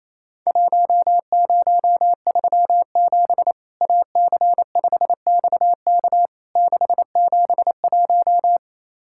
Morse code, heard as a keyed tone, about 28 wpm, 700 Hz. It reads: 1037 AC5XK 671
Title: Worked All Europe DX Contest - QTC Exchange - 28wpm